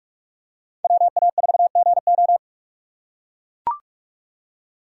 Morse code, heard as a keyed tone, about 40 wpm, 700 Hz. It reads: WA4GO